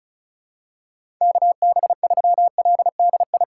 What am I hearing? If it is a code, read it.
Morse code, heard as a keyed tone, about 34 wpm, 700 Hz. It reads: KB3LDI
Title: USA Call Signs - 2x Character Spacing - 35wpm